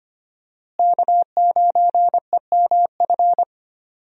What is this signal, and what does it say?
Morse code, heard as a keyed tone, about 25 wpm, 700 Hz. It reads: K9EMF
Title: USA Call Signs - 5 Characters - Repeated 2x - 25wpm